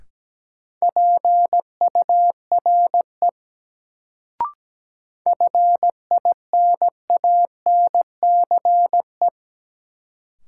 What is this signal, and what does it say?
Morse code, heard as a keyed tone, about 17 wpm, 700 Hz. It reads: PURE FINANCE
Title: Sentences from Top 1500 Words - New Word Drill - 17wpm